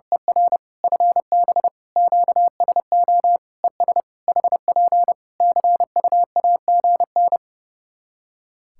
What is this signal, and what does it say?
Morse code, heard as a keyed tone, about 30 wpm, 700 Hz. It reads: ER FB QSO ES HP CUAGN
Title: K1USN SST QSO Practice 30wpm